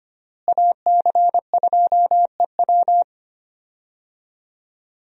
Morse code, heard as a keyed tone, about 25 wpm, 700 Hz. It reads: AC2EW
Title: USA Call Signs - 5 Characters - Repeated 3x - 25wpm